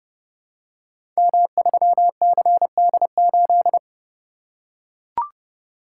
Morse code, heard as a keyed tone, about 32 wpm, 700 Hz. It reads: M3CD8
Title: Code Groups - 5 Letter:Numbers - 30wpm